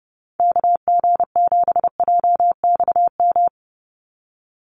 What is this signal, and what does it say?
Morse code, heard as a keyed tone, about 30 wpm, 700 Hz. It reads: KG7JXM
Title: USA Call Signs 30wpm